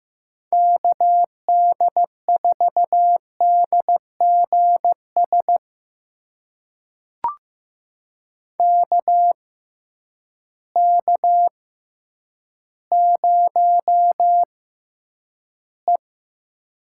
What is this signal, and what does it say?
Morse code, heard as a keyed tone, about 15 wpm, spaced as for 9 wpm, 700 Hz. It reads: KD4DGS KK0E
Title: USA Call Signs - 6x Character Spacing - 15wpm